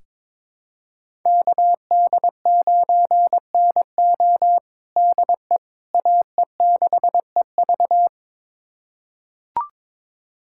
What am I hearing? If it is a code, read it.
Morse code, heard as a keyed tone, about 22 wpm, 700 Hz. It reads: KD9NO DE AE6EV
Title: USA Callsign DE Callsign - 5 Characters - 22wpm